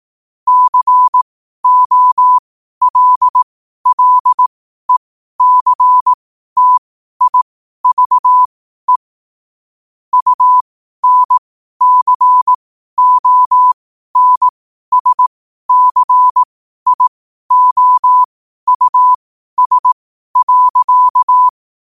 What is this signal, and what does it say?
Morse code, heard as a keyed tone, about 18 wpm, spaced as for 12 wpm, 1000 Hz. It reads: COLLECTIVE UNCONSCIOUS.
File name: qotdfiles / 12wpm